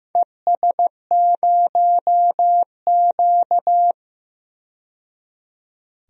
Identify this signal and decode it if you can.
Morse code, heard as a keyed tone, about 15 wpm, 700 Hz. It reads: ES0Q
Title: DXCC-100 Call Signs - 4 Characters - 15wpm